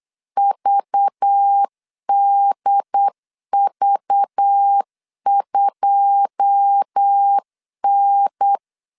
Morse code, heard as a keyed tone, about 8 wpm, 800 Hz. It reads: VDV2N
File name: CW / DET / 1 9 1500C S A 8wpm 5637